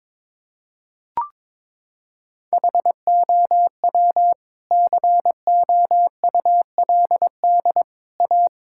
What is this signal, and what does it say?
Morse code, heard as a keyed tone, about 22 wpm, 700 Hz. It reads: HOW COULD A
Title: Sentences from Top 1000 Words - Review Part 6 - 22wpm